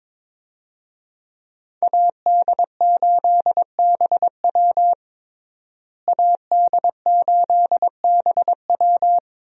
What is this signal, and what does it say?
Morse code, heard as a keyed tone, about 21 wpm, 700 Hz. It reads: AD8BW AD8BW
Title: USA Call Signs - 5 Characters - Repeated 2x - 22wpm